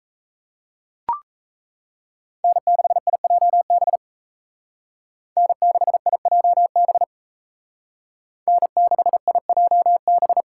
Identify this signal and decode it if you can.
Morse code, heard as a keyed tone, about 34 wpm, 700 Hz. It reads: N6IJB N6IJB N6IJB
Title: Speed Racing - USA 1x3 Call Signs 28wpm